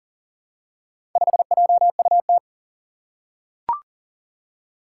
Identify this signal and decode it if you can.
Morse code, heard as a keyed tone, about 40 wpm, 700 Hz. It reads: 5JUT
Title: Four Challenging Letter-Number Combinations 40wpm